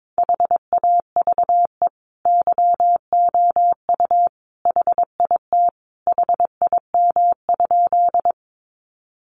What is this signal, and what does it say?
Morse code, heard as a keyed tone, about 22 wpm, 700 Hz. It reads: HAVE YOU HIT HIM?
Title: Sentences from Top 1200 Words 22wpm